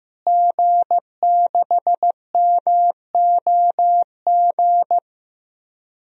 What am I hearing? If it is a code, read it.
Morse code, heard as a keyed tone, about 15 wpm, 700 Hz. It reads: G6MOG
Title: DXCC-100 Call Signs - 5 Characters - Repeated 3x - 15wpm